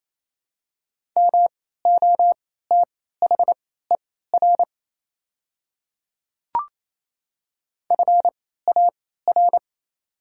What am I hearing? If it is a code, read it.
Morse code, heard as a keyed tone, about 27 wpm, spaced as for 16 wpm, 700 Hz. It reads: MOTHER FAR
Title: 3x Character Spacing - Top 300 Words - 28wpm